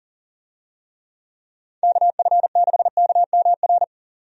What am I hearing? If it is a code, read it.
Morse code, heard as a keyed tone, about 40 wpm, 700 Hz. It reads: KF6KMR
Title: USA Call Signs - 6 Characters - 40wpm